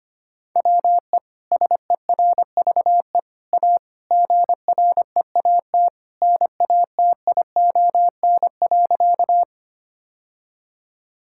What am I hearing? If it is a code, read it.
Morse code, heard as a keyed tone, about 25 wpm, 700 Hz. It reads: WE SERVE A GREAT NATION.
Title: Sentences from Top 1000 Words - Review Part 6 - 25wpm